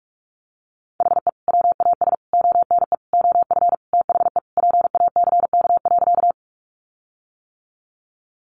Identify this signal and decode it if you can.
Morse code, heard as a keyed tone, about 44 wpm, 700 Hz. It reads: HE WAS ONE OF THE PACK.
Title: Sentences from Top 1300 Words 45wpm